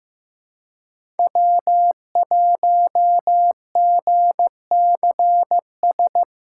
Morse code, heard as a keyed tone, about 15 wpm, 700 Hz. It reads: W1GCS
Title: K1USN SST QSO Practice - 15wpm